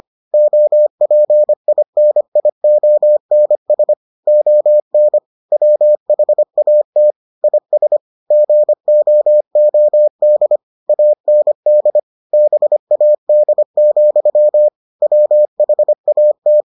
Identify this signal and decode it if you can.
Morse code, heard as a keyed tone, about 25 wpm, 600 Hz. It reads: OPINIONS ON WHAT IS GOOD AND BAD, WHAT